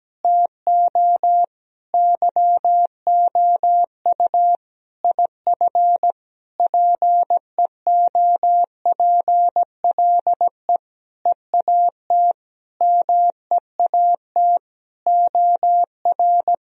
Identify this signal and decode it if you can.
Morse code, heard as a keyed tone, about 17 wpm, 700 Hz. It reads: TO YOU IF PEOPLE EAT MEAT OR
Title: Sentences from Top 1000 Words - Review Part 3 - 17wpm